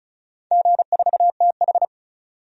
Morse code, heard as a keyed tone, about 36 wpm, 700 Hz. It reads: G4TH
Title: DXCC-100 Call Signs - 4 Characters - 35wpm